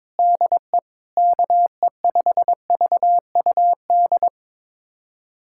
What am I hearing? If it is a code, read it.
Morse code, heard as a keyed tone, about 22 wpm, 700 Hz. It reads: DE KE5VUD
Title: USA Callsign DE Callsign - 6 Characters - 22wpm